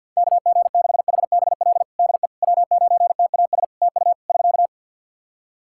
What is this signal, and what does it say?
Morse code, heard as a keyed tone, about 50 wpm, 700 Hz. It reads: KG6HBL DE W9TAS TU <SK>
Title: Practice QSO - Exchange 4 - Ending - 50wpm